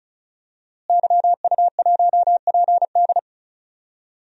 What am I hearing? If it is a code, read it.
Morse code, heard as a keyed tone, about 36 wpm, 700 Hz. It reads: YU1PD
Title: DXCC-100 Call Signs - 5 Characters - Repeated 2x - 35wpm